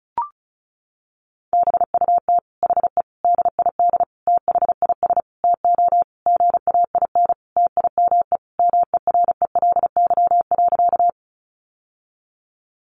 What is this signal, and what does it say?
Morse code, heard as a keyed tone, about 34 wpm, 700 Hz. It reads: BUT HE DID THIS TO GAIN TIME MERELY.
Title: Sentences from Top 1400 Words 35wpm